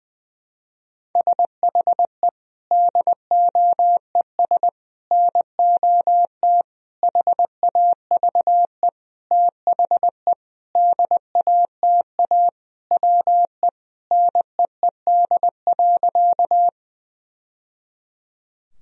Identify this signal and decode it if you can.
Morse code, heard as a keyed tone, about 21 wpm, 700 Hz. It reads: SHE DOES NOT HAVE THE DATA WE NEED.
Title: Sentences from Top 1000 Words - Review Part 3 - 20wpm